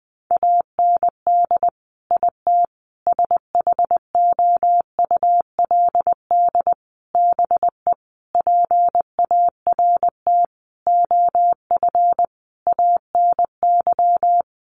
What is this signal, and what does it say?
Morse code, heard as a keyed tone, about 20 wpm, 700 Hz. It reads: AND IT SHOULD BE PART OF ANY
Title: Sentences from Top 1500 Words 20wpm